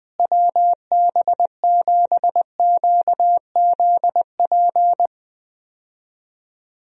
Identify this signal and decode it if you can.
Morse code, heard as a keyed tone, about 20 wpm, 700 Hz. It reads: WB7QZP
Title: USA Call Signs - 6 Characters - Repeated 2x - 20wpm